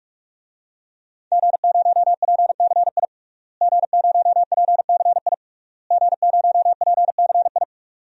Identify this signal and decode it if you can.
Morse code, heard as a keyed tone, about 46 wpm, 700 Hz. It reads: G0PCI G0PCI G0PCI
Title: DXCC-100 Call Signs - 5 Characters - Repeated 3x - 45wpm